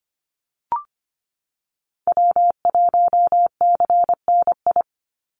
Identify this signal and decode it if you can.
Morse code, heard as a keyed tone, about 25 wpm, 700 Hz. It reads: W1CNI
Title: USA Call Signs 25wpm